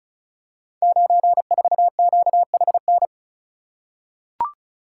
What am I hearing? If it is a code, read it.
Morse code, heard as a keyed tone, about 36 wpm, 700 Hz. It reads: 94QHN
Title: Code Groups - 5 Letter:Numbers - 35wpm